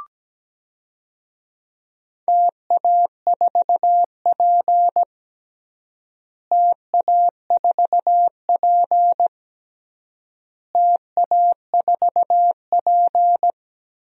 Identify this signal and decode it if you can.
Morse code, heard as a keyed tone, about 17 wpm, 700 Hz. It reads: TA4P TA4P TA4P
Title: DXCC-100 Call Signs - 4 Characters - Repeated 3x - 17wpm